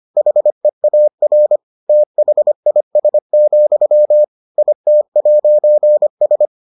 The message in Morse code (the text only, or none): HEAR THIS, IT'S